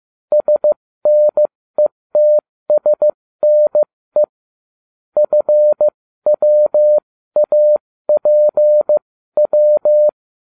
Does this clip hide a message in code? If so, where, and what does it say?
Yes: Morse code from 0.3 to 10.1 s, heard as a keyed tone, about 15 wpm, spaced as for 13 wpm, 600 Hz. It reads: SNETSNE FWAPW